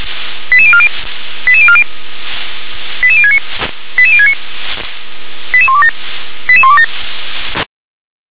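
Five-tone selective call (ZVEI), heard as quick runs of five tones, tones of about 70 ms. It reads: 88040 88040 88069 88069 88127 88127
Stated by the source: ZVEI five-tone selective call received by SDR.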